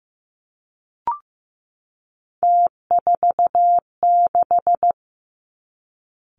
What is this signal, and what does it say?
Morse code, heard as a keyed tone, about 15 wpm, 700 Hz. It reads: T46
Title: Three Challenging Letter-Number Combinations 15wpm